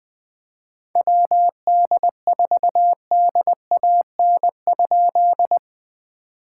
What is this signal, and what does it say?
Morse code, heard as a keyed tone, about 20 wpm, 700 Hz. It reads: WD4DAN?
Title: POTA QSO Practice - 20wpm